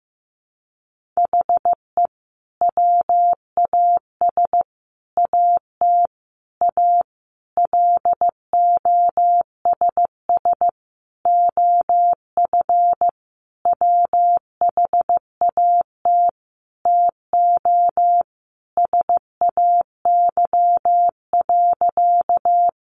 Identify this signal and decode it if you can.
Morse code, heard as a keyed tone, about 15 wpm, 700 Hz. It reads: HE WAS AT A LOSS OF WHAT TO SAY.
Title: Sentences from Top 1400 Words 15wpm